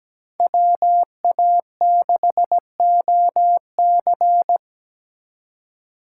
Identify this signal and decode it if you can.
Morse code, heard as a keyed tone, about 20 wpm, 700 Hz. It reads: WA6OC
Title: Speed Racing - USA 2x2 Call Signs 17wpm